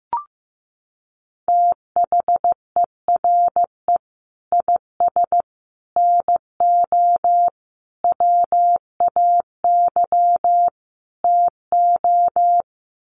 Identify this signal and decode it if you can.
Morse code, heard as a keyed tone, about 15 wpm, 700 Hz. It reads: THERE IS NO WAY TO
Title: Sets of 5 Words Encore 15wpm